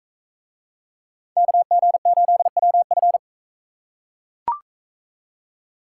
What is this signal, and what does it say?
Morse code, heard as a keyed tone, about 42 wpm, 700 Hz. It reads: KG8WF